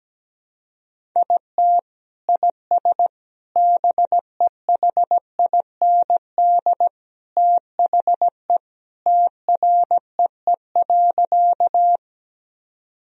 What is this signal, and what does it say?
Morse code, heard as a keyed tone, about 17 wpm, 700 Hz. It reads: IT IS BEHIND THE TREE.
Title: Sentences from Top 1000 Words - Review Part 1 - 17wpm